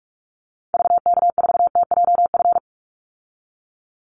Morse code, heard as a keyed tone, about 44 wpm, 700 Hz. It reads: VK4TJF
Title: Call Signs Repeated 3x - 45wpm